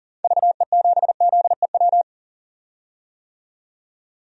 Morse code, heard as a keyed tone, about 40 wpm, 700 Hz. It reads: VE7ZEW
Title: DXCC-100 Call Signs - 6 Characters - 40wpm